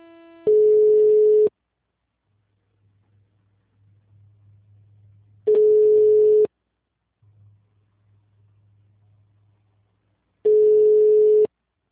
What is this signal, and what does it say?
Ringback tone (Europe, 425 Hz).
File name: out- -2031- - - .242